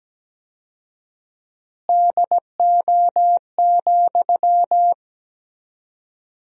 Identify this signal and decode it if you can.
Morse code, heard as a keyed tone, about 17 wpm, 700 Hz. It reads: DO,
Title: 4x Word-Spacing - Sentences from Top 100 Words - 17wpm